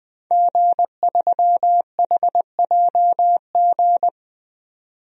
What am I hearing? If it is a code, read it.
Morse code, heard as a keyed tone, about 20 wpm, 700 Hz. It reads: G3HJG
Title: DXCC-100 Call Signs - 5 Characters - 20wpm